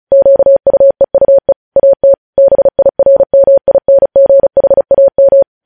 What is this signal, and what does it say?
Morse code, heard as a keyed tone, about 36 wpm, 550 Hz. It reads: QUEUE AT BIRMINGHAM
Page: news headlines in morse code at 35 wpm